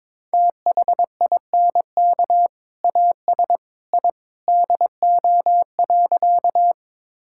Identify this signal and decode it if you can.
Morse code, heard as a keyed tone, about 22 wpm, 700 Hz. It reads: THINK AS I DO.
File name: practice / 22wpm / Sentences with 2-Word Phrases - 22wpm